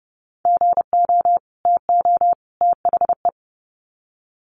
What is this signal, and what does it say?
Morse code, heard as a keyed tone, about 30 wpm, 700 Hz. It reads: GO TO THE
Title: Sets of 3 Words 30wpm